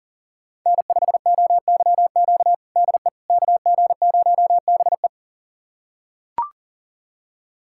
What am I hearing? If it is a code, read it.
Morse code, heard as a keyed tone, about 40 wpm, 700 Hz. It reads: N5OYQ DE KG0BE
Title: USA Callsign DE Callsign 40wpm